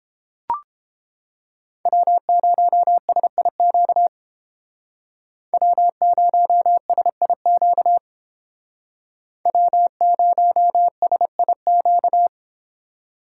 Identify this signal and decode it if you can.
Morse code, heard as a keyed tone, about 30 wpm, 700 Hz. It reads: W0SIQ W0SIQ W0SIQ